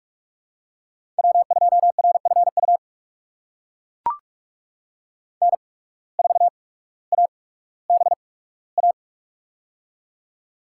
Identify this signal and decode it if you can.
Morse code, heard as a keyed tone, about 44 wpm, 690 Hz. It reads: W 2 R F U N 4 A B A
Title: USA Call Signs - 8x Character Spacing - 45wpm